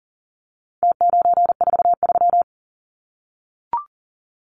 Morse code, heard as a keyed tone, about 40 wpm, 700 Hz. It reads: T943
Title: Numbers - Contest Style - 4-Digit 40wpm